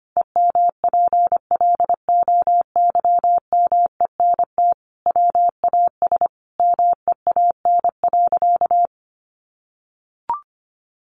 Morse code, heard as a keyed tone, about 25 wpm, 700 Hz. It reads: EMPLOYMENT WAS MEAN.
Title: Sentences from Top 1500 Words 25wpm